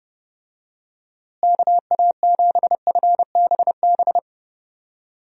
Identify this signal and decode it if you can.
Morse code, heard as a keyed tone, about 32 wpm, 700 Hz. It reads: KA7FBB
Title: USA Call Signs - 6 Characters - Repeated 2x - 30wpm